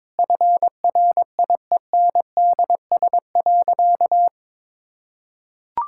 Morse code, heard as a keyed tone, about 21 wpm, 700 Hz. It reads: FRIENDS.
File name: Difficult Sentences with Hints - 22wpm